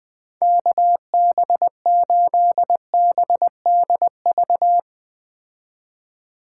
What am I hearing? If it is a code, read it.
Morse code, heard as a keyed tone, about 20 wpm, 700 Hz. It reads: KB8BDV
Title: Speed Racing - USA 2x3 Call Signs 17wpm